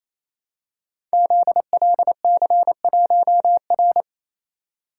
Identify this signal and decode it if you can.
Morse code, heard as a keyed tone, about 27 wpm, 700 Hz. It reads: ZLC1R
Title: Code Groups - 5 Letter:Numbers - 28wpm